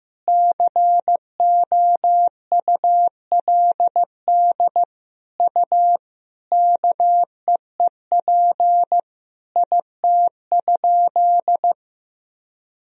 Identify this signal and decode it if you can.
Morse code, heard as a keyed tone, about 15 wpm, 700 Hz. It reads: COULD U KEEP IT?
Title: Sentences from Top 300 Words and Abbreviations 15wpm